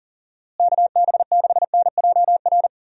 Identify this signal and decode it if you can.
Morse code, heard as a keyed tone, about 40 wpm, 700 Hz. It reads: KB6NJR
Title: USA Call Signs - 6 Characters - Repeated 3x - 40wpm